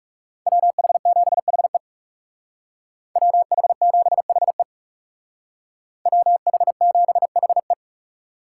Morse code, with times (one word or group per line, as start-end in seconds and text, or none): 0.5-1.8 WH7HE
3.2-4.6 WH7HE
6.1-7.7 WH7HE